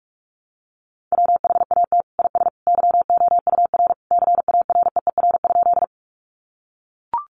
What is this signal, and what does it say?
Morse code, heard as a keyed tone, about 44 wpm, 700 Hz. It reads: WHAT IS YOUR CAREER?
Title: Sentences from Top 1000 Words 45wpm